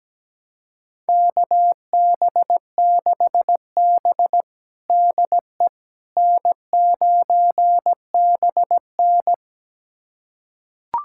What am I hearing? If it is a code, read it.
Morse code, heard as a keyed tone, about 17 wpm, 700 Hz. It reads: KB6B DE N9BN
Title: USA Callsign DE Callsign - 4 Characters - 17wpm